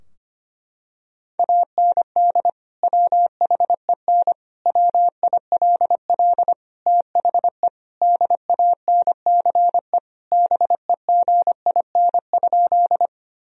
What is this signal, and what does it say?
Morse code, heard as a keyed tone, about 25 wpm, 700 Hz. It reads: AND WHEN WILL THE DANCE BEGIN?
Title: Sentences from Top 1000 Words - Review Part 4 - 25wpm